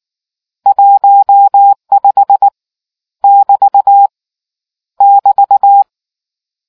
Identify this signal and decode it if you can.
Morse code, heard as a keyed tone, about 19 wpm, 800 Hz. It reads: 15 <BT> <BT>